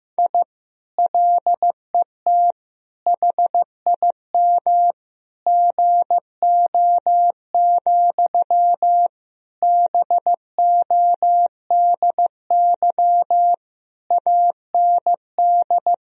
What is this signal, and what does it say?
Morse code, heard as a keyed tone, about 15 wpm, 700 Hz. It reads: I LET HIM GO, BODY AND
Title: Sentences from Top 1000 Words - Review Part 2 - 15wpm